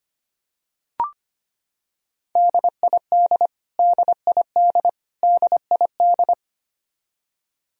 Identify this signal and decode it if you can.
Morse code, heard as a keyed tone, about 25 wpm, 700 Hz. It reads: DID DID DID
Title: Top 100 Words Repeated 25wpm